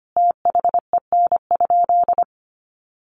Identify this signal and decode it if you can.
Morse code, heard as a keyed tone, about 25 wpm, 700 Hz. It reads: THEN?
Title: Sentences from Top 800 Words 25wpm